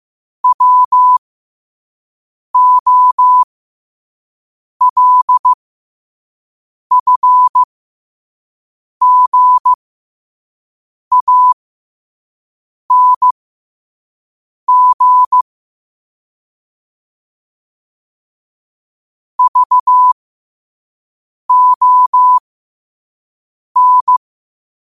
Morse code, heard as a keyed tone, about 15 wpm, spaced as for 5 wpm, 1000 Hz. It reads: WOLFGANG VON
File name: qotdfiles / 5wpm